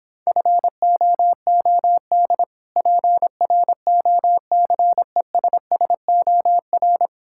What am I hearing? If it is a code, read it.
Morse code, heard as a keyed tone, about 26 wpm, 700 Hz. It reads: FOOD PROCESSOR